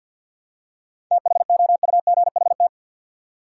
Morse code, heard as a keyed tone, about 53 wpm, 700 Hz. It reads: THOUGHT
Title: Seven Letter Words - 50wpm